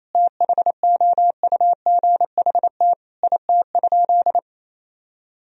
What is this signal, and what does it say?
Morse code, heard as a keyed tone, about 28 wpm, 700 Hz. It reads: THOUGHT IT?
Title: Sentences from Top 300 Words and Abbreviations 28wpm